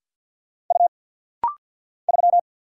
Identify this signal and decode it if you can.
Morse code, heard as a keyed tone, about 50 wpm, 700 Hz. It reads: U 3
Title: Single Letter-Number - Rapid-Fire 50wpm